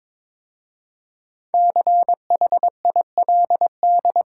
Morse code, heard as a keyed tone, about 22 wpm, 700 Hz. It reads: CHILD
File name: 5 Letter Words - First 3 Letters Given - 22wpm